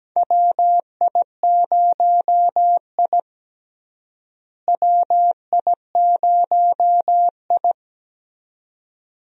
Morse code, heard as a keyed tone, about 17 wpm, 700 Hz. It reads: WI0I WI0I
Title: USA Call Signs - 4 Characters - Repeated 2x - 17wpm